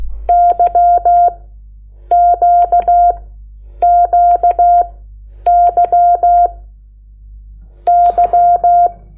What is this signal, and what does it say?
Morse code, heard as a keyed tone, about 16 wpm, 700 Hz. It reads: Y Q Q Y Y
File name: Lecciones / CLASE Nº8